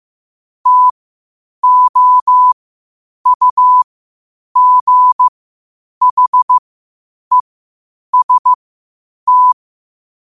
Morse code, heard as a keyed tone, about 15 wpm, spaced as for 8 wpm, 1000 Hz. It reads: TOUGHEST
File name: qotdfiles / 8wpm